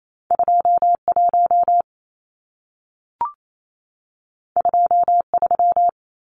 Morse code, heard as a keyed tone, about 28 wpm, 700 Hz. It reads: 21 23
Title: Numbers - 2-Digit 28wpm